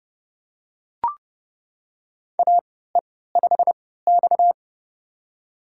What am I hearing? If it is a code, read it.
Morse code, heard as a keyed tone, about 30 wpm, spaced as for 17 wpm, 700 Hz. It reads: AE5X
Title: USA Call Signs - 4 Characters - 3x Character Spacing - 30wpm